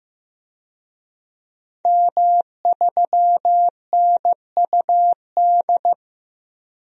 Morse code, heard as a keyed tone, about 15 wpm, 700 Hz. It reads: M3NUD
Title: DXCC-100 Call Signs - 5 Characters - 15wpm